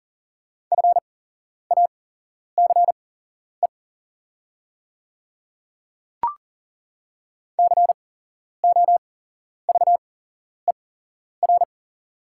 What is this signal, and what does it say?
Morse code, heard as a keyed tone, about 40 wpm, spaced as for 11 wpm, 700 Hz. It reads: FACE COVER
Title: 8x Character Spacing - Top 500 Words - 40wpm